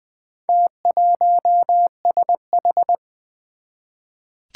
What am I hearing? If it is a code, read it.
Morse code, heard as a keyed tone, about 20 wpm, 700 Hz. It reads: T1SH
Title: Four Challenging Letter-Number Combinations 20wpm